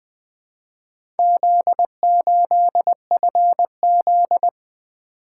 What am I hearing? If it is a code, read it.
Morse code, heard as a keyed tone, about 20 wpm, 700 Hz. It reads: Z8FZ
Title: Four Letter-Number Combinations - 20wpm